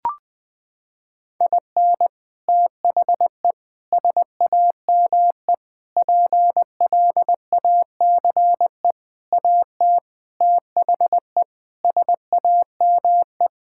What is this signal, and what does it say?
Morse code, heard as a keyed tone, about 21 wpm, 700 Hz. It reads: IN THE SAME PLACE AT THE SAME
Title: Sets of 8 Words - 20wpm